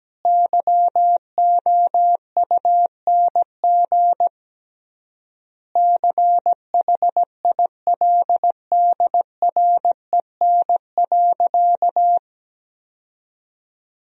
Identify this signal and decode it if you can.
Morse code, heard as a keyed tone, about 17 wpm, 700 Hz. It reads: YOUNG CHILDREN.
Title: 3x Word-Spacing - Sentences from Top 400 Words - 17wpm